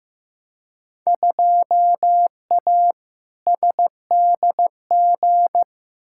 Morse code, heard as a keyed tone, about 15 wpm, 700 Hz. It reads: 2A SDG
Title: Field Day Exchanges Encore - 15wpm